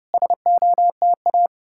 Morse code, heard as a keyed tone, about 34 wpm, 700 Hz. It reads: SOTA